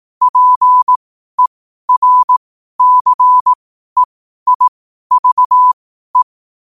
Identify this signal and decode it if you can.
Morse code, heard as a keyed tone, about 18 wpm, spaced as for 12 wpm, 1000 Hz. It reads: PERCEIVE
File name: qotdfiles / 12wpm